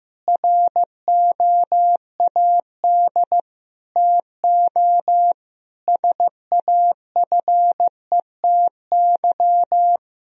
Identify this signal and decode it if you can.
Morse code, heard as a keyed tone, about 16 wpm, 700 Hz. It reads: ROAD TO SAFETY
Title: Sentences from Top 1000 Words - Review Part 4 - 15wpm